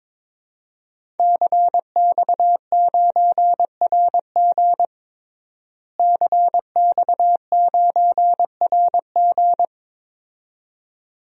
Morse code, heard as a keyed tone, about 22 wpm, 700 Hz. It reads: CX9RG CX9RG
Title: DXCC-100 Call Signs - 5 Characters - Repeated 2x - 22wpm